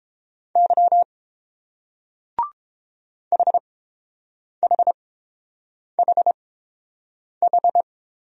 Morse code, heard as a keyed tone, about 30 wpm, spaced as for 7 wpm, 690 Hz. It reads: Y HHHH